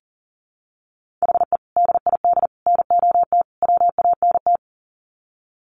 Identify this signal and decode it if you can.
Morse code, heard as a keyed tone, about 37 wpm, 700 Hz. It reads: HE DID NOT WANT
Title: Sets of 4 Words 40wpm